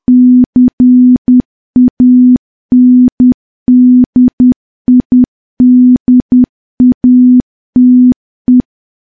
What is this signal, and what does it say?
Morse code, heard as a keyed tone, about 10 wpm, 250 Hz. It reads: CANDIDATE
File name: morse / morse-training-10WPM